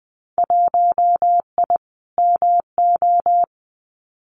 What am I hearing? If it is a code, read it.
Morse code, heard as a keyed tone, about 20 wpm, 700 Hz. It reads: 1I MO
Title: Winter Field Day Exchanges Encore 20wpm